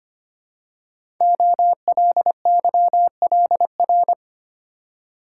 Morse code, heard as a keyed tone, about 25 wpm, 700 Hz. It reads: OLYLR
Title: Code Groups - 5 Letter:Numbers - 25wpm